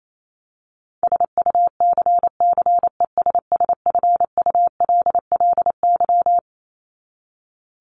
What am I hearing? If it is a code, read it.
Morse code, heard as a keyed tone, about 28 wpm, 700 Hz. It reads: SUCCESSFULLY
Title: Top 1501-1750 Words 28wpm